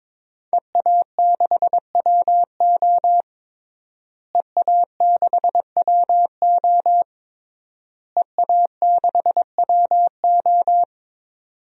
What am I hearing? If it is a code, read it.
Morse code, heard as a keyed tone, about 22 wpm, 700 Hz. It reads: EA6WO EA6WO EA6WO
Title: DXCC-100 Call Signs - 5 Characters - Repeated 3x - 22wpm